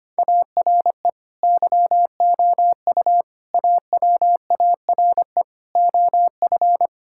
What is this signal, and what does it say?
Morse code, heard as a keyed tone, about 25 wpm, 700 Hz. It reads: ARE YOU AWARE OF
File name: practice / 25wpm / Sentences with 2-Word Phrases - 25wpm